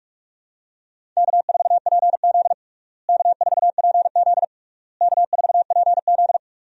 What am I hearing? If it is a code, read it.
Morse code, heard as a keyed tone, about 46 wpm, 700 Hz. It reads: K4PZ K4PZ K4PZ
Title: USA Call Signs - 4 Characters - Repeated 3x - 45wpm